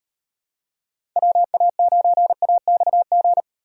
Morse code, heard as a keyed tone, about 38 wpm, 700 Hz. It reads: WA9AXG